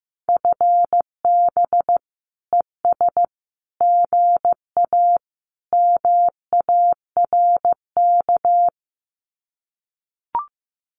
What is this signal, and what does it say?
Morse code, heard as a keyed tone, about 14 wpm, 700 Hz. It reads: FB ES GA MARK
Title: Common QSO Phrases Encore 15wpm